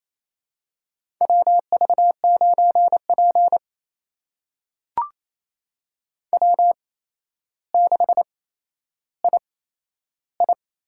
Morse code, heard as a keyed tone, about 28 wpm, spaced as for 12 wpm, 700 Hz. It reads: WV9P W6II
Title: USA Call Signs - 4 Characters - 8x Character Spacing - 28wpm